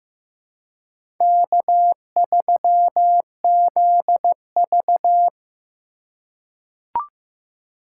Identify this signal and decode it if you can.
Morse code, heard as a keyed tone, about 15 wpm, 700 Hz. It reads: K3ZV
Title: USA Call Signs - 4 Characters - 4x Character Spacing - 15wpm